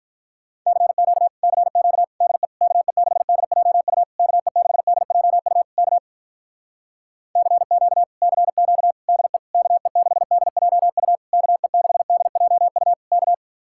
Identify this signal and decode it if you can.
Morse code, heard as a keyed tone, about 46 wpm, 700 Hz. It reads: CQ CQ DE KE6DJU KE6DJU K CQ CQ DE KE6DJU KE6DJU K